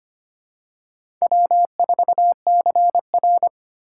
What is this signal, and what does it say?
Morse code, heard as a keyed tone, about 25 wpm, 700 Hz. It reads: W4CR
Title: USA Call Signs - 4 Characters - 4x Character Spacing - 25wpm